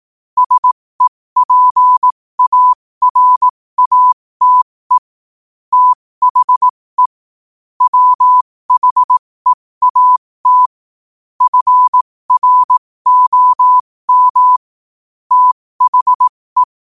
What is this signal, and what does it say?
Morse code, heard as a keyed tone, about 18 wpm, spaced as for 15 wpm, 1000 Hz. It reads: SEPARATE THE WHEAT FROM THE
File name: qotdfiles / 15wpm